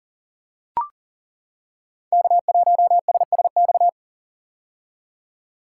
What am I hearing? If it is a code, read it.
Morse code, heard as a keyed tone, about 41 wpm, 700 Hz. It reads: K1SSX
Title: Code Groups - 5 Letter:Numbers - 40wpm